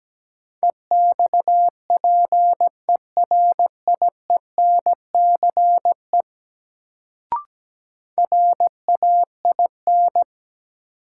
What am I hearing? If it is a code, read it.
Morse code, heard as a keyed tone, about 17 wpm, 700 Hz. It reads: EXPERIENCE RAIN
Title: Sentences from Top 600 Words - New Word Drill - 17wpm